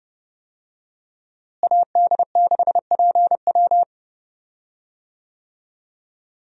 Morse code, heard as a keyed tone, about 30 wpm, 700 Hz. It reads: AD6PW
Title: USA Call Signs - 5 Characters - Repeated 2x - 30wpm